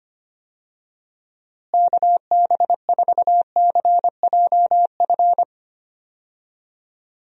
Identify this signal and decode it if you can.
Morse code, heard as a keyed tone, about 25 wpm, 700 Hz. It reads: KB4CJF
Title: USA Call Signs - 6 Characters - Repeated 2x - 25wpm